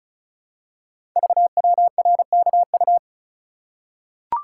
Morse code, heard as a keyed tone, about 36 wpm, 700 Hz. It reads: VWRKU
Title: Code Groups - 5 Letter:Numbers - 35wpm